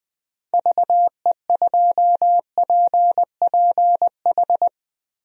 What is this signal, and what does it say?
Morse code, heard as a keyed tone, about 21 wpm, 700 Hz. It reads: VE2PPH
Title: DXCC-100 Call Signs - 6 Characters - Repeated 3x - 20wpm